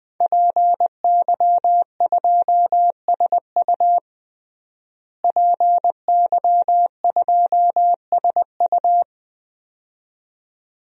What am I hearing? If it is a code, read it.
Morse code, heard as a keyed tone, about 21 wpm, 700 Hz. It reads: PY2SU PY2SU
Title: DXCC-100 Call Signs - 5 Characters - Repeated 2x - 20wpm